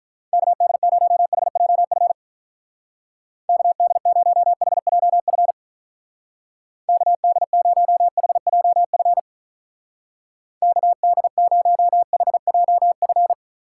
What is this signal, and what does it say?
Morse code, heard as a keyed tone, about 46 wpm, 700 Hz. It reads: KD0HJF KD0HJF KD0HJF KD0HJF